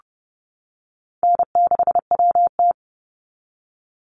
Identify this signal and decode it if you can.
Morse code, heard as a keyed tone, about 30 wpm, 700 Hz. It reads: N6WT
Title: NA SOTA Chaser Callsigns 30wpm